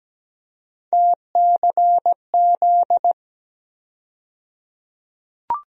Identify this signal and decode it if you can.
Morse code, heard as a keyed tone, about 17 wpm, 700 Hz. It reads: TCZ
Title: USA Call Sign Postfixes - 3 Letters 17wpm